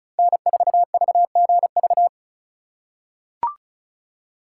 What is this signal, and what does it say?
Morse code, heard as a keyed tone, about 36 wpm, 700 Hz. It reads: N4VGV
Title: USA Call Signs - 5 Characters - 3x Character Spacing - 35wpm